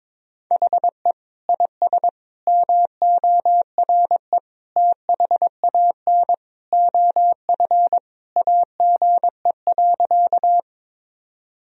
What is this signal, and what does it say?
Morse code, heard as a keyed tone, about 22 wpm, 700 Hz. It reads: HE IS MORE THAN OF AGE.
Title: Sentences from Top 1000 Words - Review Part 3 - 22wpm